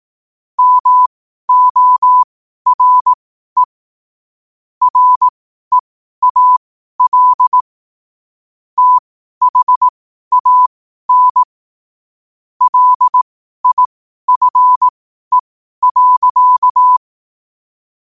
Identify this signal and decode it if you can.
Morse code, heard as a keyed tone, about 18 wpm, spaced as for 12 wpm, 1000 Hz. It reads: MORE REAL THAN LIFE.
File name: qotdfiles / 12wpm